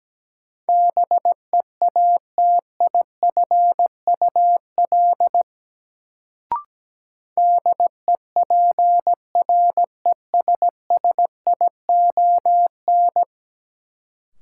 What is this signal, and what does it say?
Morse code, heard as a keyed tone, about 17 wpm, 700 Hz. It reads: BEATIFUL DEPRESSION
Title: Sentences from Top 1300 Words - New Word Drill Alt - 17wpm